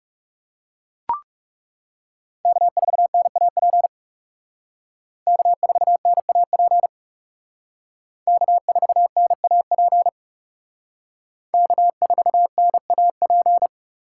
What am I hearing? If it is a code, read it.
Morse code, heard as a keyed tone, about 36 wpm, 700 Hz. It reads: K4NAP K4NAP K4NAP K4NAP
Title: Speed Racing - USA 1x3 Call Signs 30wpm